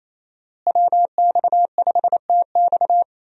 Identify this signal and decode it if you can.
Morse code, heard as a keyed tone, about 27 wpm, 700 Hz. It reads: WX5TX
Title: USA Call Signs - 5 Characters - Repeated 3x - 28wpm